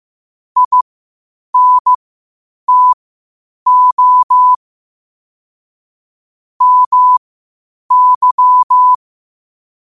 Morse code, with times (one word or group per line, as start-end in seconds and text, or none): 0.6-4.5 INTO
6.6-8.9 MY